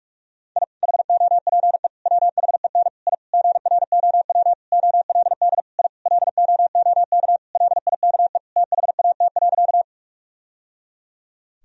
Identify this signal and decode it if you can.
Morse code, heard as a keyed tone, about 41 wpm, 700 Hz. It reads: I HOPE WHEN I GROW OLD I LOOK LIKE THAT.
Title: Sentences from Top 1000 Words - Review Part 1 - 45wpm